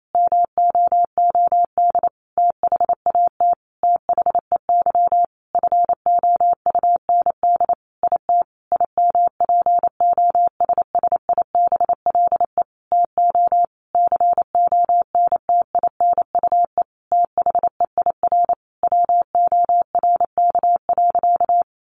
Morse code, heard as a keyed tone, about 28 wpm, 700 Hz. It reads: MOOD THAT THEY FOUND IT IMPOSSIBLE TO CONTINUE THEIR WORK.
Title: Sentences from Top 1100 Words 28wpm